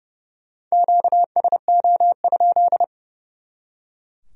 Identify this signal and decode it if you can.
Morse code, heard as a keyed tone, about 38 wpm, 700 Hz. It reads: QSO?